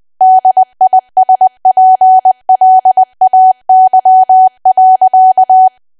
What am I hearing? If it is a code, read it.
Morse code, heard as a keyed tone, about 20 wpm, 750 Hz. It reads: DISPLAY.
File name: cwa / W1AW-Level3 / 20WPM